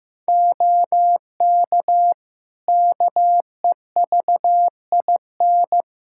Morse code, heard as a keyed tone, about 15 wpm, 700 Hz. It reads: OK KEVIN
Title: Practice QSO - Exchange 3 - More Info - 15wpm